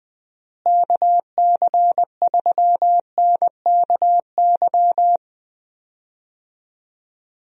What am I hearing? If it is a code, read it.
Morse code, heard as a keyed tone, about 20 wpm, 700 Hz. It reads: KC3NKY
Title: USA Call Signs - 6 Characters - Repeated 3x - 20wpm